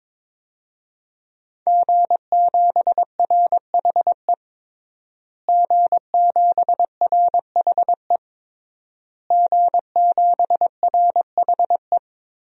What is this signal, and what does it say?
Morse code, heard as a keyed tone, about 22 wpm, 700 Hz. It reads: G7RHE G7RHE G7RHE
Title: DXCC-100 Call Signs - 5 Characters - Repeated 3x - 22wpm